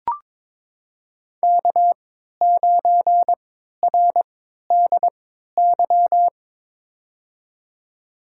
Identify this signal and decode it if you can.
Morse code, heard as a keyed tone, about 22 wpm, spaced as for 12 wpm, 700 Hz. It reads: K9RDY
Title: USA Call Signs - 3x Character Spacing - 22wpm